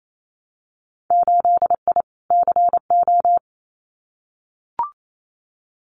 Morse code, heard as a keyed tone, about 28 wpm, 700 Hz. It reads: 8I CO
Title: Winter Field Day Exchanges Encore 28wpm